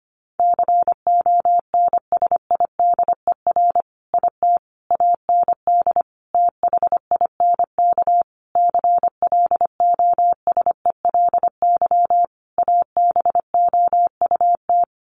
Morse code, heard as a keyed tone, about 25 wpm, 700 Hz. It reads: CONSIDER IT AND THINK CLOSELY ABOUT
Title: Sentences from Top 1400 Words 25wpm